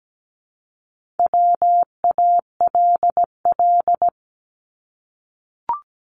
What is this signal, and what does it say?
Morse code, heard as a keyed tone, about 17 wpm, 700 Hz. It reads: WALL
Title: Multi-letter Phonemes Repeated 3x 17wpm